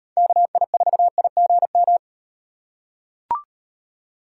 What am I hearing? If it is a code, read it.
Morse code, heard as a keyed tone, about 38 wpm, 700 Hz. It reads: KI4IGM